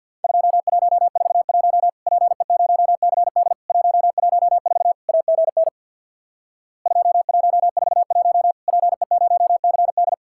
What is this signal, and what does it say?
Morse code, heard as a keyed tone, about 50 wpm, 700 Hz. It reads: 2141 PE0CD 114 AGN 2141 PE0CD
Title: Worked All Europe DX Contest - QTC Exchange with Repeats - 50wpm